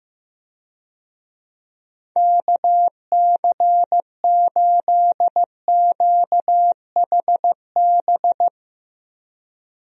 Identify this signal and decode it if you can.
Morse code, heard as a keyed tone, about 15 wpm, 700 Hz. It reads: KC8QHB
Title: USA Call Signs - 6 Characters - Repeated 3x - 15wpm